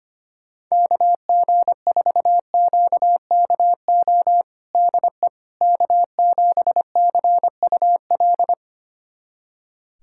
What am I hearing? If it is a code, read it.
Morse code, heard as a keyed tone, about 25 wpm, 700 Hz. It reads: KG4QKO DE K7CUL
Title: USA Callsign DE Callsign 25wpm